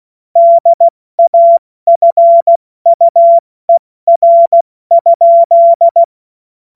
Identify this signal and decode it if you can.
Morse code, heard as a keyed tone, about 16 wpm, 680 Hz. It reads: DAFUER?
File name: Morsekurs / Witze / Witz 03. August 2021, 70 F80